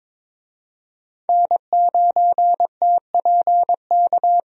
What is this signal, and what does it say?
Morse code, heard as a keyed tone, about 21 wpm, 700 Hz. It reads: N9TPK
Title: USA Call Signs - 5 Characters - 22wpm